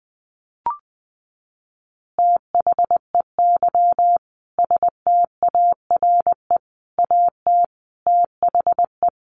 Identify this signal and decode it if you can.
Morse code, heard as a keyed tone, about 20 wpm, 700 Hz. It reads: THEY STARE AT THE
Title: Easy Phrases 20wpm